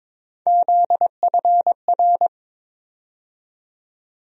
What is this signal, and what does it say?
Morse code, heard as a keyed tone, about 22 wpm, 700 Hz. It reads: ZFR
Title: USA Call Sign Postfixes - 3 Letters 22wpm